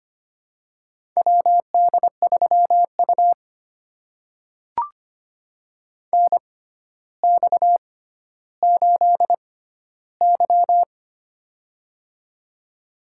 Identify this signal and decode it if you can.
Morse code, heard as a keyed tone, about 25 wpm, spaced as for 13 wpm, 700 Hz. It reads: WD3U NX8Y
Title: USA Call Signs - 4 Characters - 6x Character Spacing - 25wpm